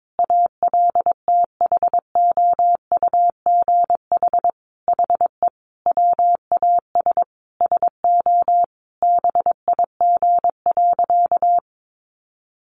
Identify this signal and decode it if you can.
Morse code, heard as a keyed tone, about 22 wpm, 700 Hz. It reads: ALTHOUGH HE WAS SO BIG.
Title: Sentences from Top 600 Words 22wpm